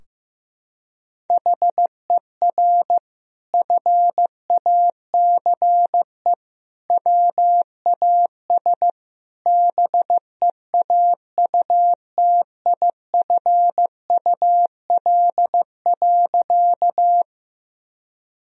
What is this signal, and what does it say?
Morse code, heard as a keyed tone, about 15 wpm, 700 Hz. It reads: HER FACE WAS BEAUTIFUL.
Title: Sentences from Top 1000 Words - Review Part 2 - 15wpm